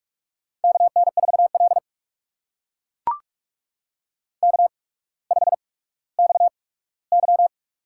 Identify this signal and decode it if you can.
Morse code, heard as a keyed tone, about 44 wpm, spaced as for 19 wpm, 700 Hz. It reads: KN4L K5XY
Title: USA Call Signs - 4 Characters - 8x Character Spacing - 45wpm